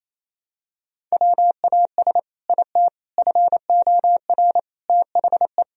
Morse code, heard as a keyed tone, about 28 wpm, 700 Hz. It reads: WAS IT FOR THE
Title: Sentences from Top 1000 Words - Review Part 3 - 28wpm